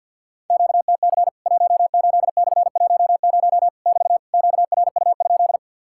Morse code, heard as a keyed tone, about 50 wpm, 700 Hz. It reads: QTC 18/10 <BT> QRU?
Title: Worked All Europe DX Contest - QTC Exchange with Repeats - 50wpm